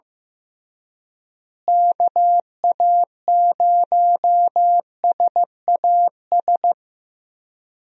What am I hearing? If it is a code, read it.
Morse code, heard as a keyed tone, about 14 wpm, 700 Hz. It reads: KA0SAS
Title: USA Call Signs - 6 Characters - Repeated 3x - 15wpm